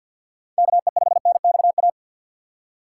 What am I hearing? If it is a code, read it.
Morse code, heard as a keyed tone, about 47 wpm, 700 Hz. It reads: KE5NXA
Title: USA Call Signs - 6 Characters - 6x Character Spacing - 50wpm